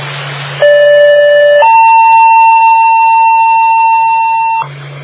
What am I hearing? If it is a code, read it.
Two-tone page, heard as a one-second tone, then a three-second tone, about 580 then 900 Hz. It is heard from 0.6 to 4.6 s.